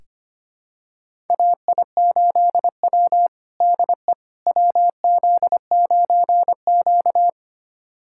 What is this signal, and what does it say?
Morse code, heard as a keyed tone, about 25 wpm, 700 Hz. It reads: AI8W DE WZ9Q
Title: USA Callsign DE Callsign - 4 Characters - 25wpm